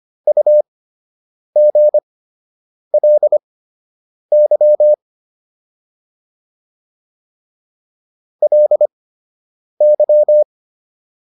Morse code, heard as a keyed tone, about 25 wpm, spaced as for 8 wpm, 600 Hz. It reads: UGLY LY